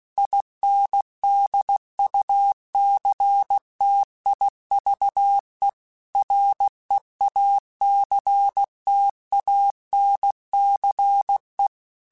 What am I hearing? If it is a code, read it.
Morse code, heard as a keyed tone, about 16 wpm, 800 Hz. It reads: INDUCTIVE REACTANCE
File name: Technical Subjects / Technical Subjects-mp3s / XL @15